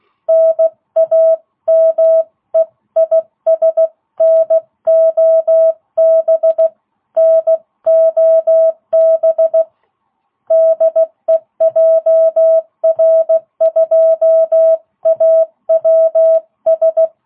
Morse code, heard as a keyed tone, about 16 wpm, 660 Hz. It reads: NAME IS NOB NOB DE JR2AWS